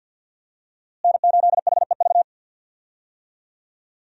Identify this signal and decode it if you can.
Morse code, heard as a keyed tone, about 50 wpm, 700 Hz. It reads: N8HEV
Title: USA Call Signs - 5 Characters - Repeated 3x - 50wpm